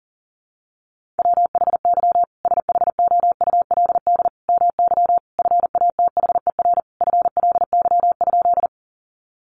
Morse code, heard as a keyed tone, about 40 wpm, 700 Hz. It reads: WHY SHOULD MY FATHER FLY?
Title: Sentences from Top 500 Words 40wpm